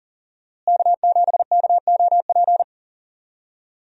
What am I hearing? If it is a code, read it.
Morse code, heard as a keyed tone, about 40 wpm, 700 Hz. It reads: K7KOP
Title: USA Call Signs - 5 Characters - Repeated 2x - 40wpm